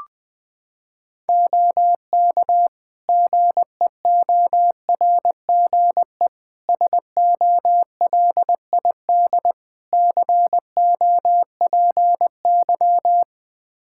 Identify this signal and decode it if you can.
Morse code, heard as a keyed tone, about 20 wpm, 700 Hz. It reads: OK GEORGE SOLID COPY
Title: K1USN SST QSO Practice 20wpm